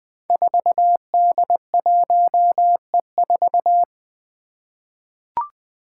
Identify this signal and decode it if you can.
Morse code, heard as a keyed tone, about 20 wpm, 700 Hz. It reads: 4D1E4
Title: Five Challenging Letter-Number Combinations 20wpm